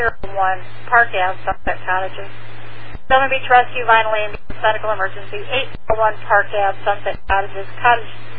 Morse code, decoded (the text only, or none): none